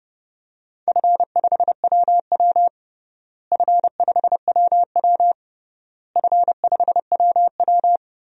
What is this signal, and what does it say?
Morse code, heard as a keyed tone, about 30 wpm, 700 Hz. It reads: F5WW F5WW F5WW
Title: DXCC-100 Call Signs - 4 Characters - Repeated 3x - 30wpm